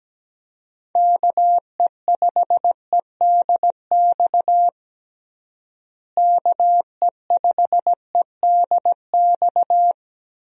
Morse code, heard as a keyed tone, about 17 wpm, 700 Hz. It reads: KE5EDX KE5EDX
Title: USA Call Signs - 6 Characters - Repeated 2x - 17wpm